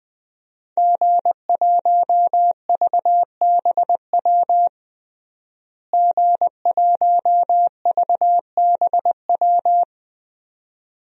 Morse code, heard as a keyed tone, about 20 wpm, 700 Hz. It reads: G1VBW G1VBW
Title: DXCC-100 Call Signs - 5 Characters - Repeated 2x - 20wpm